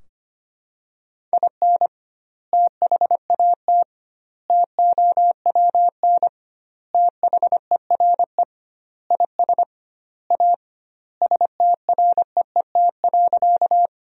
Morse code, heard as a keyed tone, about 25 wpm, 700 Hz. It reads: IN THAT TOWN THERE IS A STREET.
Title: 2x Word-Spacing - Sentences from Top 500 Words - 25wpm